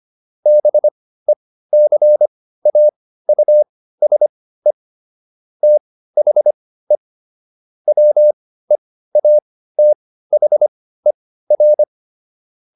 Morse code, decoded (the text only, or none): BECAUSE THE WEATHER